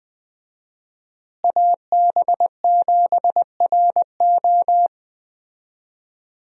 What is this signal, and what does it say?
Morse code, heard as a keyed tone, about 20 wpm, 700 Hz. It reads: AB7RO
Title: USA Call Signs - 5 Characters - Repeated 3x - 20wpm